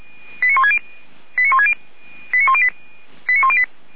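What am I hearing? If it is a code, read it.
Five-tone selective call (ZVEI), heard as quick runs of five tones, tones of about 70 ms. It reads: 89160 89108 89108
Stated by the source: ZVEI five-tone selective call 89336 received by SDR.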